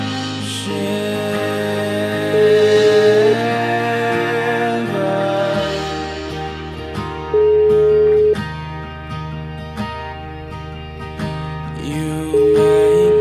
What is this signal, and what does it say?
Ringback tone (Europe, 425 Hz), with noise.